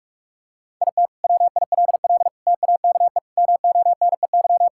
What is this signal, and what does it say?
Morse code, heard as a keyed tone, about 44 wpm, 700 Hz. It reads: IT WILL TAKE MONEY
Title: Sentences from Top 1600 Words 45wpm